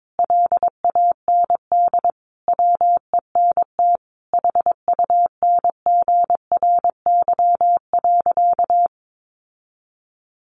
Morse code, heard as a keyed tone, about 22 wpm, 700 Hz. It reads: LAND WENT HUNGRY.
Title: Sentences from Top 1400 Words 22wpm